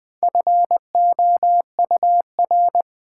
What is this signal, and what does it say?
Morse code, heard as a keyed tone, about 20 wpm, 700 Hz. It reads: FOUR
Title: Sentences from Top 300 Words - New Word Drill - 20wpm